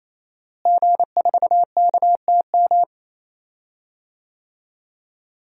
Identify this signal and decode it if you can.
Morse code, heard as a keyed tone, about 27 wpm, 700 Hz. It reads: G4KTM
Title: DXCC-100 Call Signs - 5 Characters - Repeated 3x - 28wpm